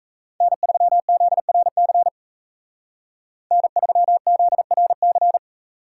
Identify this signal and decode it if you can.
Morse code, heard as a keyed tone, about 35 wpm, 700 Hz. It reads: N3ZRC N3ZRC
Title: Speed Racing - USA 1x3 Call Signs 28wpm